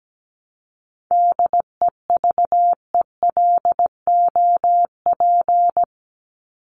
Morse code, heard as a keyed tone, about 17 wpm, 700 Hz. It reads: DEVELOP
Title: ED Suffix - Top 100 Words with Base Words 17wpm